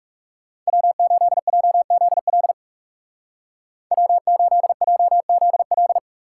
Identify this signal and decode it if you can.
Morse code, heard as a keyed tone, about 37 wpm, 700 Hz. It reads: W8JZL W8JZL
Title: Speed Racing - USA 1x3 Call Signs 30wpm